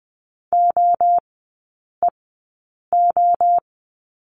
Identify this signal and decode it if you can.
Morse code, heard as a keyed tone, about 20 wpm, spaced as for 8 wpm, 700 Hz. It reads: OEO
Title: 035 Lesson 7e - Letter s - Three Characters 20wpm